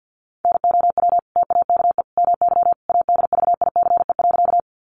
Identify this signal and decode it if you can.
Morse code, heard as a keyed tone, about 50 wpm, 700 Hz. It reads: NOW TAKE MY ADVICE.
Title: Sentences from Top 1100 Words 50wpm